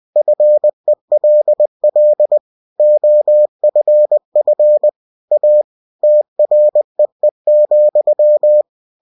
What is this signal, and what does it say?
Morse code, heard as a keyed tone, about 20 wpm, 600 Hz. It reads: FELL OFF A TREE,
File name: winnie-20wpm